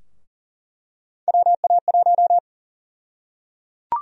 Morse code, heard as a keyed tone, about 37 wpm, 700 Hz. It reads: WA1
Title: USA Call Sign Prefixes - 40wpm